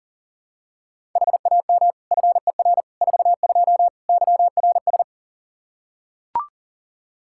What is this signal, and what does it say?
Morse code, heard as a keyed tone, about 40 wpm, 700 Hz. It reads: HAM FER 42 YRS
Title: Practice QSO - Exchange 3 - More Info - 40wpm